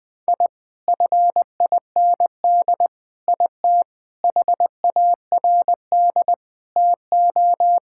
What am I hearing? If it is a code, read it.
Morse code, heard as a keyed tone, about 20 wpm, 700 Hz. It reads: I FIND IT HARD TO
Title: Sentences with 7-Word Phrases - Spoken Beginning - 20wpm